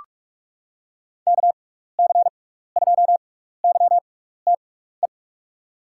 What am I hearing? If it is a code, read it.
Morse code, heard as a keyed tone, about 46 wpm, spaced as for 16 wpm, 700 Hz. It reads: KC2YTE
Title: USA Call Signs - 6 Characters - 6x Character Spacing - 45wpm